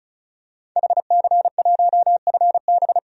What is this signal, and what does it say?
Morse code, heard as a keyed tone, about 36 wpm, 700 Hz. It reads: HC1FB
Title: DXCC-100 Call Signs - 5 Characters - 35wpm